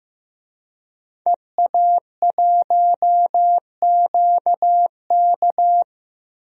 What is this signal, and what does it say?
Morse code, heard as a keyed tone, about 15 wpm, 700 Hz. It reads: EA1QK
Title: DXCC-100 Call Signs - 5 Characters - Repeated 3x - 15wpm